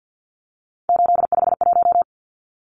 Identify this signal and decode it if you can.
Morse code, heard as a keyed tone, about 50 wpm, 700 Hz. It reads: 851
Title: Numbers - 3-Digit 50wpm